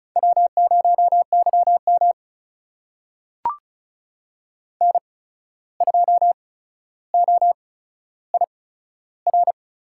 Morse code, heard as a keyed tone, about 36 wpm, spaced as for 14 wpm, 700 Hz. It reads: W0YM N2OIR
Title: USA Call Signs - 8x Character Spacing - 35wpm